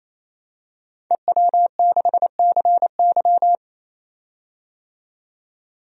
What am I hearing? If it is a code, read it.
Morse code, heard as a keyed tone, about 28 wpm, 700 Hz. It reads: EW6CY
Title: DXCC-100 Call Signs - 5 Characters - 28wpm